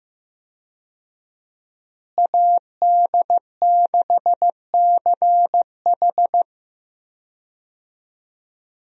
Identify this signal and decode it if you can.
Morse code, heard as a keyed tone, about 15 wpm, 700 Hz. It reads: AD6CH
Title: USA Call Signs - 5 Characters - Repeated 2x - 15wpm